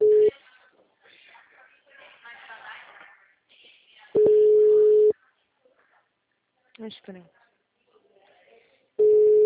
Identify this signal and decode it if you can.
Ringback tone (Europe, 425 Hz), with speech.
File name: out- -2017- - - .61895